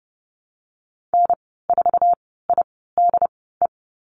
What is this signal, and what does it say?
Morse code, heard as a keyed tone, about 30 wpm, spaced as for 17 wpm, 700 Hz. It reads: N4IDE
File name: Lesson 14g - Number 2 - Call Signs 30wpm